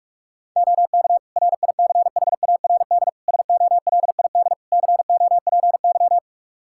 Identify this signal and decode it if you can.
Morse code, heard as a keyed tone, about 46 wpm, 700 Hz. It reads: OK RICHARD SOLID COPY
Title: Practice QSO - Exchange 3 - More Info - 45wpm